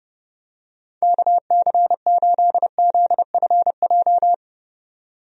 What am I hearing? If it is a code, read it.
Morse code, heard as a keyed tone, about 30 wpm, 700 Hz. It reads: KC8ZFJ
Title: USA Call Signs - 6 Characters - 30wpm